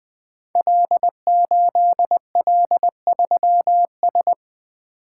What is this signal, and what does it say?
Morse code, heard as a keyed tone, about 20 wpm, 700 Hz. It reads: L8L3S
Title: Code Groups - 5 Letter:Numbers - 20wpm